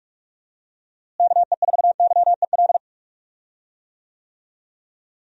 Morse code, heard as a keyed tone, about 43 wpm, 700 Hz. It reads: KE4YEL
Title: USA Call Signs - 6 Characters - Repeated 3x - 45wpm